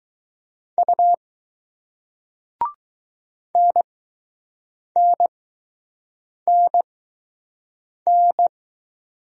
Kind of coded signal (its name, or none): Morse code